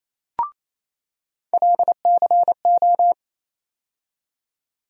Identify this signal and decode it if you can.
Morse code, heard as a keyed tone, about 28 wpm, 700 Hz. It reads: LCO
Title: USA Call Sign Postfixes - 3 Letters 28wpm